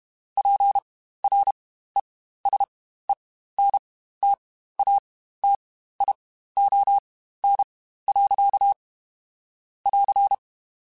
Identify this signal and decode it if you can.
Morse code, heard as a keyed tone, about 32 wpm, spaced as for 15 wpm, 800 Hz. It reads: PRESENTATION. <AR>
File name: Quotes Spaced / Quotes Spaced-mp3s / Persperation sp @30